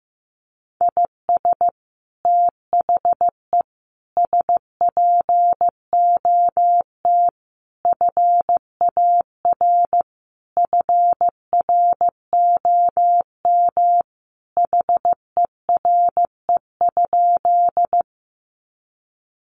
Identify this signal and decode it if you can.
Morse code, heard as a keyed tone, about 15 wpm, 700 Hz. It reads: IS THE SPOT FAR FROM HERE?
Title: Sentences from Top 1100 Words 15wpm